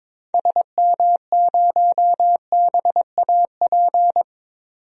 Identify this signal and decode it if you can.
Morse code, heard as a keyed tone, about 21 wpm, 700 Hz. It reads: SM0BAP
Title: DXCC-100 Call Signs - 6 Characters - 22wpm